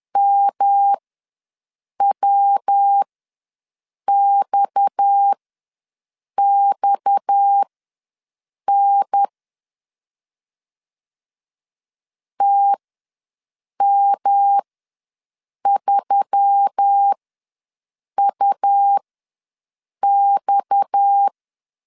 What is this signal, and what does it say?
Morse code, heard as a keyed tone, about 10 wpm, spaced as for 6 wpm, 800 Hz. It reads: MWXXN TM3UX
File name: LEZIONI / 6 PKX0 / 6 PKX ws 10wpm 3706